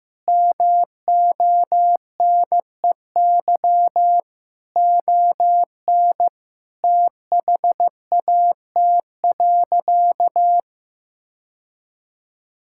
Morse code, heard as a keyed tone, about 15 wpm, 700 Hz. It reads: MONEY ON THAT.
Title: Sentences from Top 1600 Words 15wpm